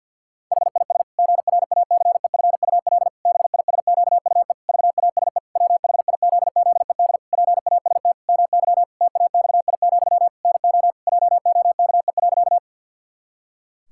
Morse code, heard as a keyed tone, about 51 wpm, 700 Hz. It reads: HIS GRACEFUL BISQUE VASE WHIZZED PAST MY TAXI, NO JOKE.
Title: Pangram Challenge v2 - 50wpm